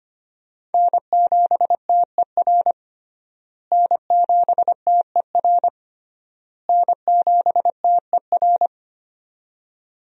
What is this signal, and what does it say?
Morse code, heard as a keyed tone, about 25 wpm, 700 Hz. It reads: N7TER N7TER N7TER
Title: USA Call Signs - 5 Characters - Repeated 3x - 25wpm